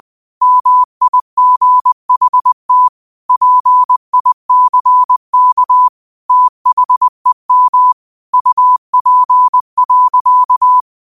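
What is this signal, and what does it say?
Morse code, heard as a keyed tone, about 20 wpm, 1000 Hz. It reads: MIGHT PICK THEM UP.
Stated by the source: Morse quote of the day sent at 20 wpm.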